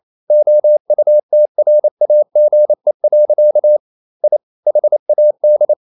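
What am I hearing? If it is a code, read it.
Morse code, heard as a keyed tone, about 28 wpm, 600 Hz. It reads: OUTRAGE. I HAD
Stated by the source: Morse code practice at 28 wpm, text of Ken Follett's novel Never in English, from CW med Gnister.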